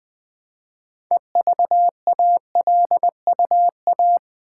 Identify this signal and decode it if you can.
Morse code, heard as a keyed tone, about 20 wpm, 700 Hz. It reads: EVALUA
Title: Build A Word - 2 Letters At A Time - 8-Letter Words - 20wpm